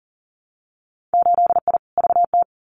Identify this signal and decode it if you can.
Morse code, heard as a keyed tone, about 42 wpm, 700 Hz. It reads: 8I VT
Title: Winter Field Day Exchanges Encore 40wpm